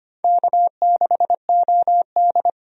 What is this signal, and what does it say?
Morse code, heard as a keyed tone, about 25 wpm, 700 Hz. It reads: K6OD
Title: USA Call Signs - 4 Characters - 25wpm